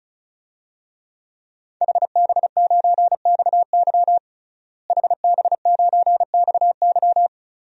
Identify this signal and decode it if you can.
Morse code, heard as a keyed tone, about 34 wpm, 700 Hz. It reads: HB9XY HB9XY
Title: DXCC-100 Call Signs - 5 Characters - Repeated 3x - 35wpm